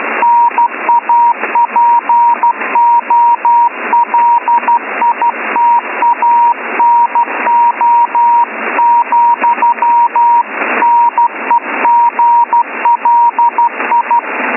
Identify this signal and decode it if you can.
Morse code, heard as a keyed tone, about 13 wpm, 950 Hz, with radio noise. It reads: NAPOLITANO, NEGLI